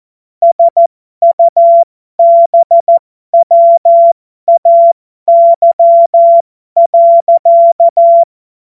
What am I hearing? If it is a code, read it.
Morse code, heard as a keyed tone, about 14 wpm, spaced as for 12 wpm, 680 Hz. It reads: SUBWAY.
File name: Morsekurs / Witze